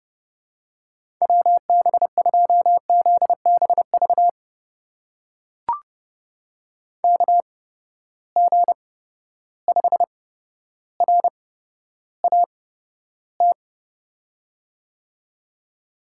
Morse code, heard as a keyed tone, about 30 wpm, spaced as for 13 wpm, 700 Hz. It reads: WB2ZBV KG5RAT
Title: USA Call Signs - 6 Characters - 8x Character Spacing - 30wpm